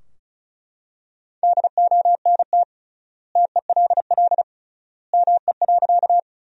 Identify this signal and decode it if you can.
Morse code, heard as a keyed tone, about 36 wpm, 700 Hz. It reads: DONT TELL ME.
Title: 3x Word-Spacing - Sentences from Top 300 Words - 35wpm